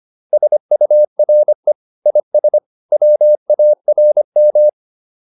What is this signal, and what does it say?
Morse code, heard as a keyed tone, about 25 wpm, 600 Hz. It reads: SURE IS WARM